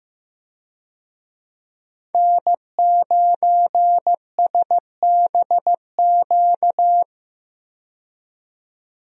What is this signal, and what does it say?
Morse code, heard as a keyed tone, about 15 wpm, 700 Hz. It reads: N9SBQ
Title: USA Call Signs - 5 Characters - 15wpm